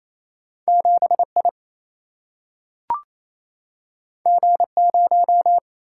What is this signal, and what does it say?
Morse code, heard as a keyed tone, about 28 wpm, 700 Hz. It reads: 7 I G 0
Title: Single-Letter Single-Number 28wpm